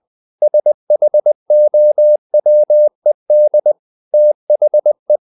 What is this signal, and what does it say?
Morse code, heard as a keyed tone, about 20 wpm, 600 Hz. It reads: SHOWED THE
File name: the old man and the sea part 2 20wpm